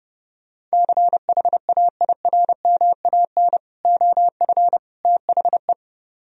Morse code, heard as a keyed tone, about 39 wpm, 700 Hz. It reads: CHAIRMAN OF THE